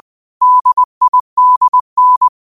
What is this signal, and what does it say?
Morse code, heard as a keyed tone, about 20 wpm, 1000 Hz. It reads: DIDN
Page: New quotes every day in morse code at 20 Words per minute.